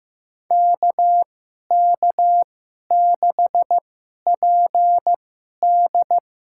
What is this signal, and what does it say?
Morse code, heard as a keyed tone, about 15 wpm, spaced as for 11 wpm, 700 Hz. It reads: KK6PD
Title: USA Call Signs - 2x Character Spacing - 15wpm